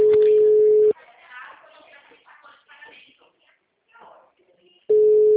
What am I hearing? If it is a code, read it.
Ringback tone (Europe, 425 Hz), with speech.
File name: out- -2007- - - .3355